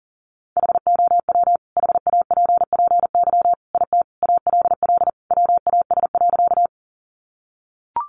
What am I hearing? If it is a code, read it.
Morse code, heard as a keyed tone, about 37 wpm, 700 Hz. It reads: HOW HAPPY IT ALL WAS.
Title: Sentences from Top 1000 Words 40wpm